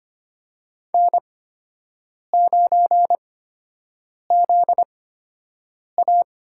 Morse code, heard as a keyed tone, about 25 wpm, spaced as for 7 wpm, 700 Hz. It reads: N9ZA
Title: USA Call Signs - 4 Characters - 8x Character Spacing - 25wpm